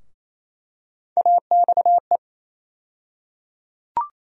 Morse code, heard as a keyed tone, about 27 wpm, 700 Hz. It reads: AXE
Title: USA Call Sign Postfixes - 3 Letters 28wpm